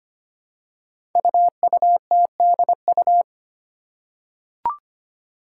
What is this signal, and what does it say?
Morse code, heard as a keyed tone, about 25 wpm, 700 Hz. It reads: UUTDU
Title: Code Groups - 5 Letters - 25wpm